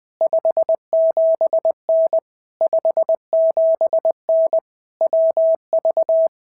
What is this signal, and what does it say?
Morse code, heard as a keyed tone, about 19 wpm, 650 Hz. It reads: 57N 57N WV
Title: POTA QSO Practice - 20wpm